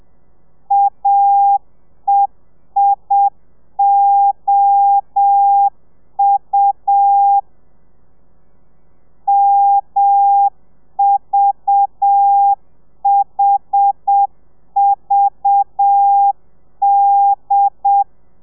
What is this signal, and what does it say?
Morse code, heard as a keyed tone, about 7 wpm, 800 Hz. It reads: AEIOU MVHVD
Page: Lezione 1: audio a 30 CPM